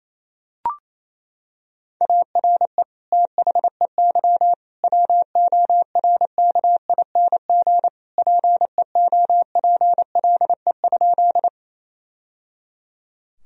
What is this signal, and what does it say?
Morse code, heard as a keyed tone, about 28 wpm, 700 Hz. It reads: ARE THEY WORKING PEOPLE?
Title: Sentences from Top 1000 Words - Review Part 4 - 28wpm